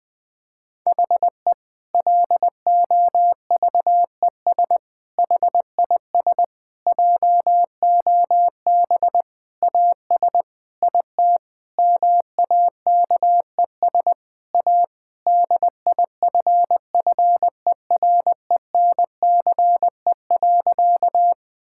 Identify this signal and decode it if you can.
Morse code, heard as a keyed tone, about 19 wpm, 700 Hz. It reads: HE LOVES HIS JOB AS IT MAKES A DIFFERENCE.
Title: Difficult Sentences with Hints - 20wpm